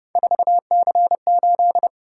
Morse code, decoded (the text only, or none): 4C8